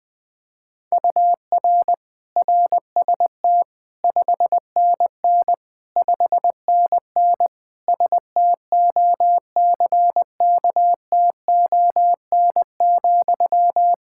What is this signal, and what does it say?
Morse code, heard as a keyed tone, about 20 wpm, 700 Hz. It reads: UR RST 5NN 5NN STOCKTON,
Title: Practice QSO - Exchange 1 - Beginning - 20wpm